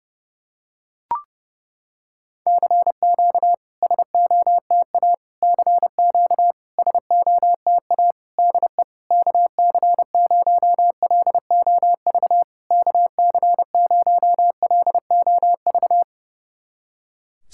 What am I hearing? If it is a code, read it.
Morse code, heard as a keyed tone, about 30 wpm, 700 Hz. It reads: CQ SOTA CQ SOTA DE KC0LOV KC0LOV
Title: SOTA QSO Practice 30wpm